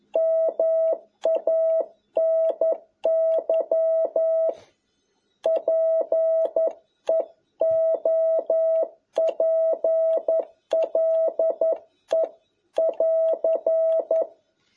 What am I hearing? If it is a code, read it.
Morse code, heard as a keyed tone, about 11 wpm, 650 Hz. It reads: MANY PEOPLE <AR>
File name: jh8cbh / 2021cw